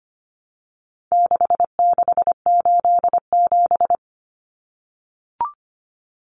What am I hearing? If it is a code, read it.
Morse code, heard as a keyed tone, about 25 wpm, 700 Hz. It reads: 6687
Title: Common Cut Numbers - 4-Digit 25wpm